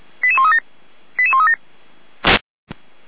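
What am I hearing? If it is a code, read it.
Five-tone selective call (ZVEI), heard as quick runs of five tones, tones of about 70 ms. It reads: 88137 88137
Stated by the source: Five-tone ZVEI selective call 88012 received on an SDR by the Feuerwehr Orlamünde fire brigade.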